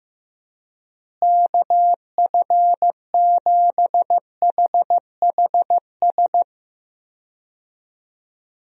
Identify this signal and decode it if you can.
Morse code, heard as a keyed tone, about 15 wpm, 700 Hz. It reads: KF7HHS
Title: USA Call Signs - 6 Characters - Repeated 2x - 15wpm